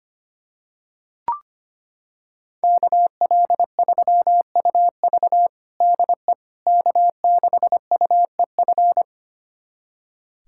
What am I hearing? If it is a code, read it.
Morse code, heard as a keyed tone, about 25 wpm, 700 Hz. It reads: KL3UV DE K6UEF
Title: USA Callsign DE Callsign - 5 Characters - 25wpm